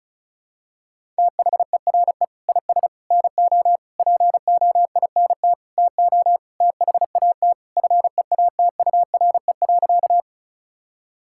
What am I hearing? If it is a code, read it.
Morse code, heard as a keyed tone, about 36 wpm, 700 Hz. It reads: THERE IS NO POINT TO THAT FEATURE.
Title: Sentences from Top 1000 Words - Review Part 5 - 35wpm